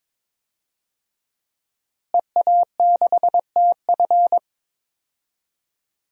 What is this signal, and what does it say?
Morse code, heard as a keyed tone, about 22 wpm, 700 Hz. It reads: EA6TF
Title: DXCC-100 Call Signs - 5 Characters - Repeated 3x - 22wpm